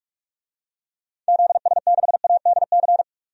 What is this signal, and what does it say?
Morse code, heard as a keyed tone, about 41 wpm, 700 Hz. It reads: ZS6ADC
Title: DXCC-100 Call Signs - 6 Characters - 45wpm